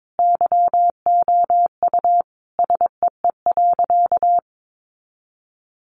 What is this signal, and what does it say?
Morse code, heard as a keyed tone, about 22 wpm, 700 Hz. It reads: YOU SEE.
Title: Sentences from Top 100 Words 22wpm